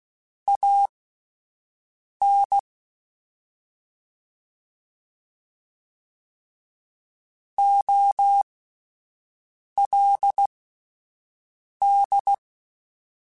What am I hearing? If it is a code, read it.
Morse code, heard as a keyed tone, about 16 wpm, spaced as for 5 wpm, 800 Hz. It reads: AN OLD
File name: (spaced) / QSOs Spaced-mp3s / WD0FBT 1 @10 sp